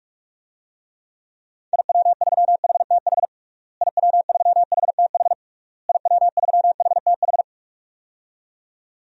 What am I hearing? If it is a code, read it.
Morse code, heard as a keyed tone, about 46 wpm, 700 Hz. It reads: IW3HTH IW3HTH IW3HTH
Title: DXCC-100 Call Signs - 6 Characters - Repeated 3x - 45wpm